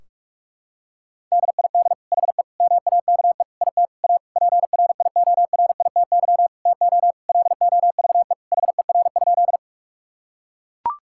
Morse code, heard as a keyed tone, about 41 wpm, 700 Hz. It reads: DID HE MAKE IT A PRIORITY TO LOVE HER?
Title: Sentences from Top 1600 Words 45wpm